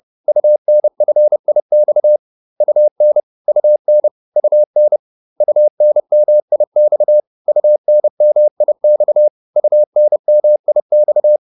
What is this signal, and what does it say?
Morse code, heard as a keyed tone, about 30 wpm, 600 Hz. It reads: UNFIX UN UN UN UNMIX UNMIX UNMIX